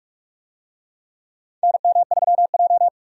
Morse code, heard as a keyed tone, about 46 wpm, 700 Hz. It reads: NM3J
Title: USA Call Signs - 4 Characters - 45wpm